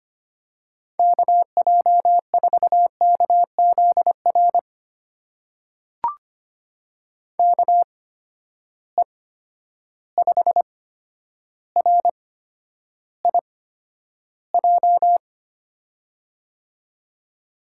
Morse code, heard as a keyed tone, about 25 wpm, spaced as for 11 wpm, 700 Hz. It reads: KJ4KZR KE5RIJ
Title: USA Call Signs - 6 Characters - 8x Character Spacing - 25wpm